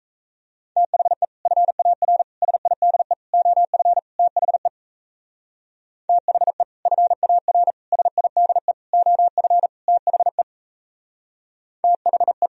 Morse code, heard as a keyed tone, about 34 wpm, 700 Hz. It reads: THE FAR SIDE OF THE THE FAR SIDE OF THE THE
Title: Speed-Racing - Sets of 5 Words 28wpm